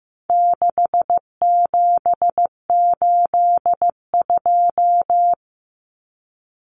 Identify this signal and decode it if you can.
Morse code, heard as a keyed tone, about 15 wpm, 700 Hz. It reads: 6782
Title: Common Cut Numbers - 4-Digit 15wpm